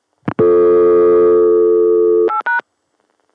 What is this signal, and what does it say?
Dial tone (North America), 350 + 440 Hz.